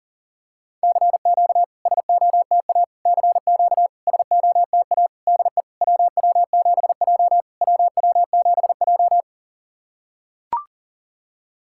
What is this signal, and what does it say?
Morse code, heard as a keyed tone, about 40 wpm, 700 Hz. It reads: CQ SOTA CQ SOTA DE WW7J WW7J
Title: SOTA QSO Practice 40wpm